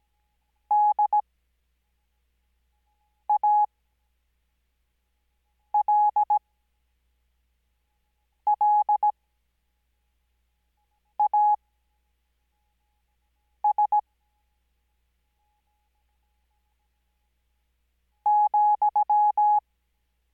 Morse code, heard as a keyed tone, about 17 wpm, spaced as for 4 wpm, 840 Hz. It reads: DALLAS ,
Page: Farnsworth Spacing: 15WPM Character speed Spaced at 5WPM.